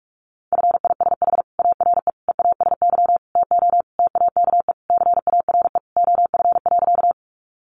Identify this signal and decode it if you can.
Morse code, heard as a keyed tone, about 44 wpm, 700 Hz. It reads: FISH ARE EASY TO TAKE CARE OF.
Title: Easy Phrases 45wpm